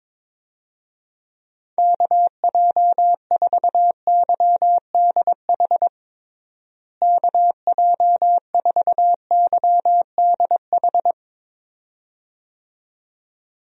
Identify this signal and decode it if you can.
Morse code, heard as a keyed tone, about 22 wpm, 700 Hz. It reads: KJ4YDH KJ4YDH
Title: USA Call Signs - 6 Characters - Repeated 2x - 22wpm